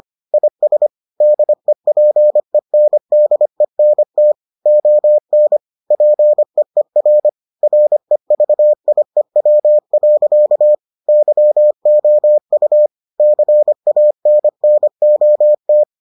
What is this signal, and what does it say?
Morse code, heard as a keyed tone, about 25 wpm, 600 Hz. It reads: IS DEPENDENT ON PEER REVIEW. YOU CANNOT